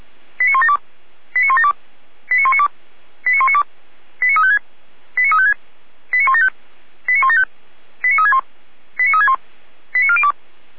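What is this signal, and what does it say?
Five-tone selective call (ZVEI), heard as quick runs of five tones, tones of about 80 ms. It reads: 89172 89172 89192 89192 89356 89356 89175 89175 89361 89361 89402
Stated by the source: ZVEI five-tone call 89018 received by SDR, channel 455.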